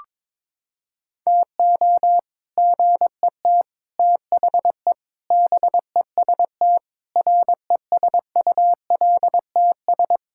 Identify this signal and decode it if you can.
Morse code, heard as a keyed tone, about 22 wpm, 700 Hz. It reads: TO GET THE BEST RESULTS
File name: practice / 22wpm / Sentences with 3-Word Phrases - 22wpm